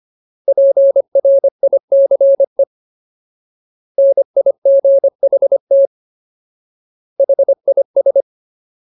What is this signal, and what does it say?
Morse code, heard as a keyed tone, about 25 wpm, 550 Hz. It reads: PRICE NIGHT HIS
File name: Ultimate Word List / 25WPM Ultimate